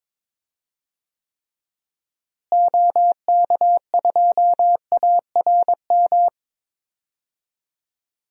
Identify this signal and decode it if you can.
Morse code, heard as a keyed tone, about 22 wpm, 700 Hz. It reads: OK2ARM
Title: DXCC-100 Call Signs - 6 Characters - Repeated 3x - 22wpm